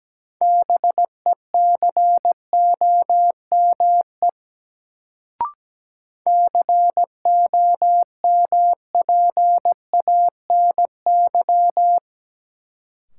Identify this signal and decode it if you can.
Morse code, heard as a keyed tone, about 17 wpm, 700 Hz. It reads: BECOME COMPANY
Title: Sentences from Top 600 Words - New Word Drill - 17wpm